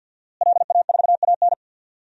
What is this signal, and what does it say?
Morse code, heard as a keyed tone, about 52 wpm, 700 Hz. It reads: LA4AN
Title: DXCC-100 Call Signs - 5 Characters - Repeated 2x - 50wpm